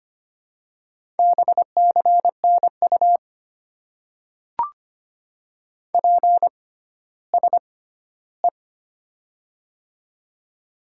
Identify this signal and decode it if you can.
Morse code, heard as a keyed tone, about 25 wpm, spaced as for 14 wpm, 690 Hz. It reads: BCNU PSE
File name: Top 90 QSO Elements - 6x Character Spacing - 25wpm